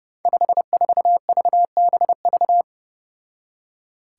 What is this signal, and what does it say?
Morse code, heard as a keyed tone, about 28 wpm, 700 Hz. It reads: 54VBV
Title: Five Challenging Letter-Number Combinations 30wpm